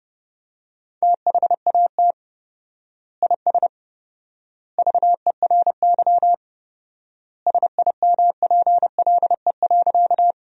Morse code, heard as a keyed tone, about 30 wpm, 700 Hz. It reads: THAT IS VERY SIMPLE.
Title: 4x Word-Spacing - Sentences from Top 400 Words - 30wpm